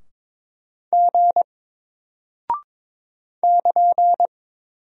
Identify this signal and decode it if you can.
Morse code, heard as a keyed tone, about 22 wpm, 700 Hz. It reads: G <KN>
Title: Single Character (Letter:Number:Punctuation:Prosign) Alt 22wpm